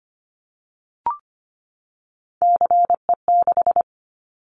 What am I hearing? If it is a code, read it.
Morse code, heard as a keyed tone, about 25 wpm, 700 Hz. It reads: CE6
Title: Callsign Prefixes for Top 100 DXCC - 25wpm